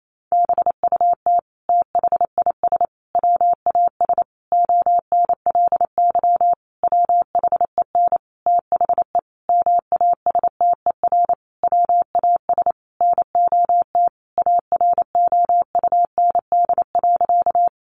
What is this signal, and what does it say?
Morse code, heard as a keyed tone, about 28 wpm, 700 Hz. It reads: BUT THIS WAS ONLY WHEN THE MASTER WAS NOT AROUND.
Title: Sentences from Top 1300 Words 28wpm